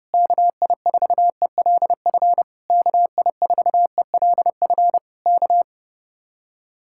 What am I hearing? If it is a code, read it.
Morse code, heard as a keyed tone, about 30 wpm, 700 Hz. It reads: KI4ELF KI4ELF K